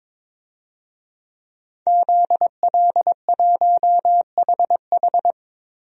Morse code, heard as a keyed tone, about 21 wpm, 700 Hz. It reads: ZL1HH
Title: DXCC-100 Call Signs - 5 Characters - 22wpm